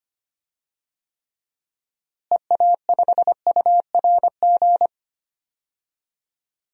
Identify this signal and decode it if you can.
Morse code, heard as a keyed tone, about 25 wpm, 700 Hz. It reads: EA5URG
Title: DXCC-100 Call Signs - 6 Characters - Repeated 3x - 25wpm